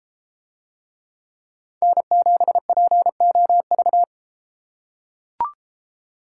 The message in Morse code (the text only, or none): N7POV